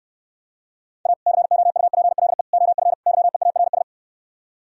Morse code, heard as a keyed tone, about 68 wpm, 700 Hz. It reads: A COUPLE OF YEARS